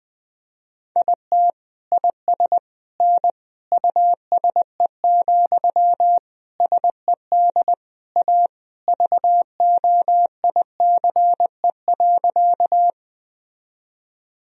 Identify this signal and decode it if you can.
Morse code, heard as a keyed tone, about 20 wpm, 700 Hz. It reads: IT IS N USE, SED A VOICE.
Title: Sentences from Top 500 Words and Abbreviations 20wpm